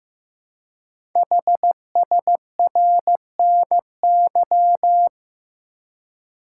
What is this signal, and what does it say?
Morse code, heard as a keyed tone, about 15 wpm, 700 Hz. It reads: HSRNY
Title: Code Groups - 5 Letters - 15wpm